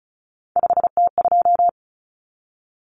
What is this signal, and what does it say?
Morse code, heard as a keyed tone, about 34 wpm, 700 Hz. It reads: 5T2
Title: Common Cut Numbers - 3-Digit 35wpm